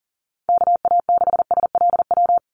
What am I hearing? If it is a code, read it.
Morse code, heard as a keyed tone, about 40 wpm, 700 Hz. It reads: KA6SLW
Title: USA Call Signs 40wpm